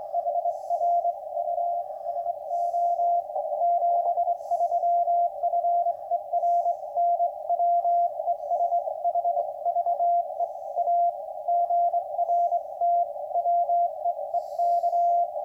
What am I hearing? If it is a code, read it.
Morse code, heard as a keyed tone, about 24 wpm, spaced as for 21 wpm, 690 Hz, with radio noise. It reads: HR U SO 73 FER NW ES HVE A GRT WEEK